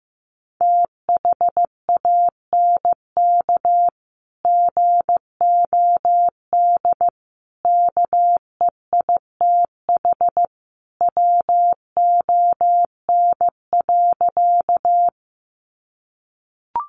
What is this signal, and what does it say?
Morse code, heard as a keyed tone, about 15 wpm, 700 Hz. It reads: THANK GOD KEITH WON.
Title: A Sound Of Thunder 15wpm